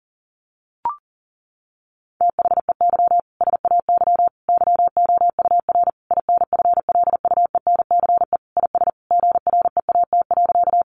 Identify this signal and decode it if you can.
Morse code, heard as a keyed tone, about 40 wpm, 700 Hz. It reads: THEY SAY YOUR INFLUENCE IS GREAT.
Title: Sentences from Top 1100 Words 40wpm